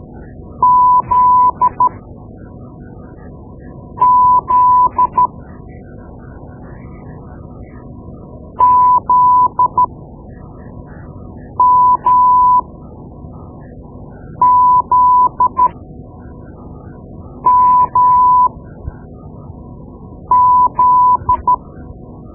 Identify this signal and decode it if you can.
Morse code, heard as a keyed tone, about 13 wpm, spaced as for 4 wpm, 1000 Hz. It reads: ZZ ZMZMZ
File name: Morse Lessons / Morse Lessons / Tape-2B